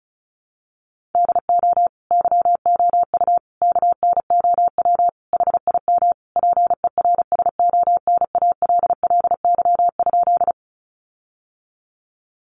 Morse code, heard as a keyed tone, about 34 wpm, 700 Hz. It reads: DO YOU KNOW HIM PERSONALLY?
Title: Sentences from Top 1400 Words 35wpm